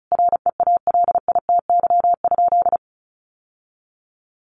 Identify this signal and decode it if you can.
Morse code, heard as a keyed tone, about 34 wpm, 700 Hz. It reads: REALITY?
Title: Sentences from Top 1200 Words 35wpm